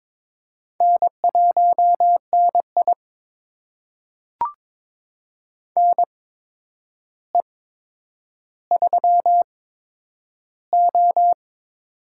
Morse code, heard as a keyed tone, about 22 wpm, spaced as for 9 wpm, 700 Hz. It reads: N1NI NE3O
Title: USA Call Signs - 4 Characters - 8x Character Spacing - 22wpm